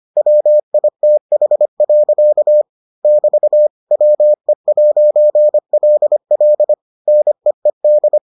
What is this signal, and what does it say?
Morse code, heard as a keyed tone, about 25 wpm, 600 Hz. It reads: WITH. <BT> WE'LL NEED